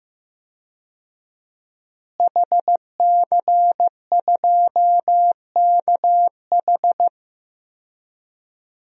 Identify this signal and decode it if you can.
Morse code, heard as a keyed tone, about 15 wpm, 700 Hz. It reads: HC2KH
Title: DXCC-100 Call Signs - 5 Characters - Repeated 2x - 15wpm